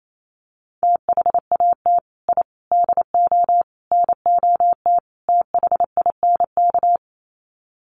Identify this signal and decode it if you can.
Morse code, heard as a keyed tone, about 28 wpm, 700 Hz. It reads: THAT I DO NOT THINK
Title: Sets of 5 Words Encore 28wpm